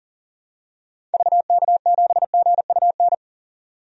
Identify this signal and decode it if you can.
Morse code, heard as a keyed tone, about 40 wpm, 700 Hz. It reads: VK7GUN
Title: DXCC-100 Call Signs - 6 Characters - 40wpm